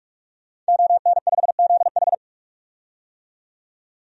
Morse code, heard as a keyed tone, about 45 wpm, 700 Hz. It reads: ON5ZH
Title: DXCC-100 Call Signs - 5 Characters - 45wpm